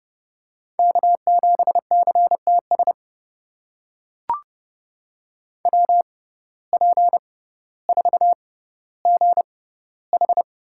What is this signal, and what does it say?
Morse code, heard as a keyed tone, about 31 wpm, spaced as for 16 wpm, 700 Hz. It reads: K7CTS WP4GH
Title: USA Call Signs - 5 Characters - 6x Character Spacing - 30wpm